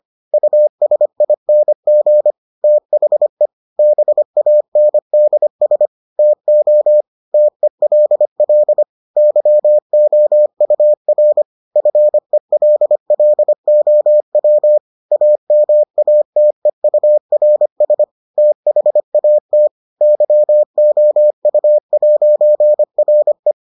Morse code, heard as a keyed tone, about 25 wpm, 600 Hz. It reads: USING THE BANDS TO TELL YOUR FELLOW AMATEURS THAT YOU'RE